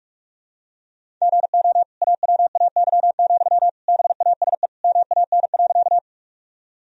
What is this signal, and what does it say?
Morse code, heard as a keyed tone, about 44 wpm, 700 Hz. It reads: GO AWAY, BASE MAN.
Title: Sentences from Top 1000 Words - Review Part 1 - 45wpm